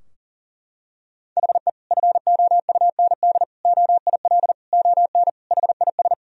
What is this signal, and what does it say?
Morse code, heard as a keyed tone, about 40 wpm, 700 Hz. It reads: HE FOUND OIL ON HIS
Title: Sentences from Top 1000 Words - Review Part 4 - 40wpm